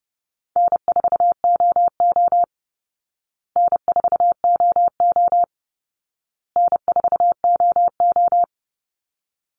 Morse code, heard as a keyed tone, about 30 wpm, 700 Hz. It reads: N4OO N4OO N4OO
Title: Easy Call Signs Repeated 3x - 30wpm